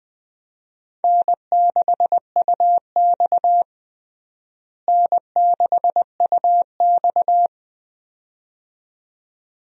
Morse code, heard as a keyed tone, about 20 wpm, 700 Hz. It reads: N6UX N6UX
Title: USA Call Signs - 4 Characters - Repeated 2x - 20wpm